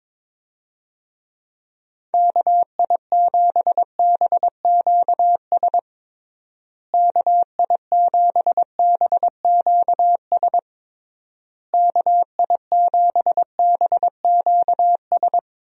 Morse code, heard as a keyed tone, about 22 wpm, 700 Hz. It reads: KI7BQS KI7BQS KI7BQS
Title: USA Call Signs - 6 Characters - Repeated 3x - 22wpm